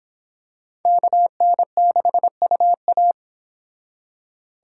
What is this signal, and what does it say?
Morse code, heard as a keyed tone, about 26 wpm, 700 Hz. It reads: KN6UA